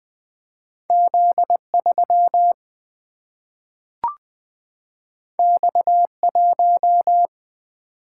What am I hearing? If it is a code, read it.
Morse code, heard as a keyed tone, about 20 wpm, 700 Hz. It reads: Z3 X1
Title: Single-Letter Single-Number 20wpm